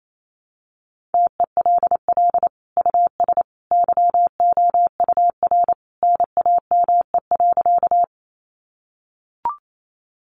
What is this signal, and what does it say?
Morse code, heard as a keyed tone, about 28 wpm, 700 Hz. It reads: TELL US YOUR NAME.
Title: Sentences from Top 200 Words 28wpm